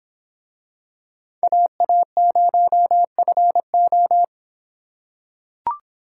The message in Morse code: AA0FO

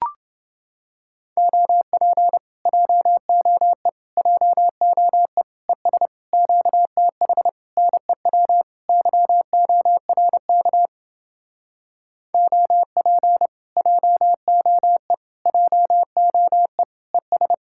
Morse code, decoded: OP JOE JOE ES QTH NEW YORK OP JOE JOE ES

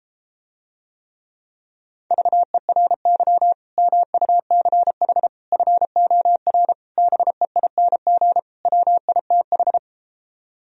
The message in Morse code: VERY MUCH FOR BEING WITH

